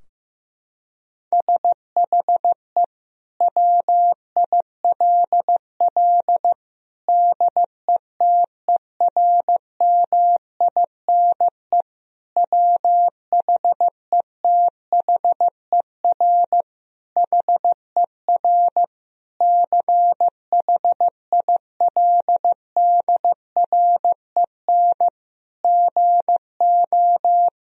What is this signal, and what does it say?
Morse code, heard as a keyed tone, about 16 wpm, 700 Hz. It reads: SHE WILL DETERMINE WHETHER HER CHILDREN GO
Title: Sentences from Top 1000 Words - Review Part 4 - 15wpm